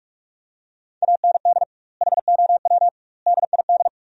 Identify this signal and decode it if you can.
Morse code, heard as a keyed tone, about 41 wpm, 700 Hz. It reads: AND HOW DID
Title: 2x Word-Spacing - Sentences from Top 400 Words - 45wpm